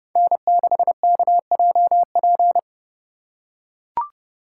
Morse code, heard as a keyed tone, about 28 wpm, 700 Hz. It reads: N6KJP